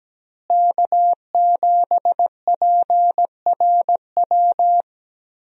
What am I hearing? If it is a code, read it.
Morse code, heard as a keyed tone, about 17 wpm, 700 Hz. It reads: K7PRW
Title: USA Call Signs - 5 Characters - Repeated 2x - 17wpm